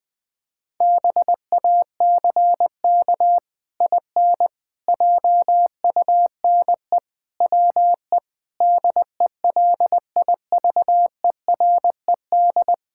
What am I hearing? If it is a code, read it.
Morse code, heard as a keyed tone, about 20 wpm, 700 Hz. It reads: BACK IN JUNE WE DELIVERED
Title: Pangram Challenge v2 - 20wpm